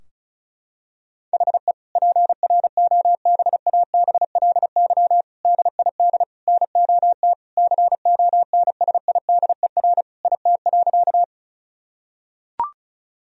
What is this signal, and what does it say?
Morse code, heard as a keyed tone, about 34 wpm, 700 Hz. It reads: HE PROBABLY DID NOT CONSIDER IT.
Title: Sentences from Top 1000 Words - Review Part 4 - 35wpm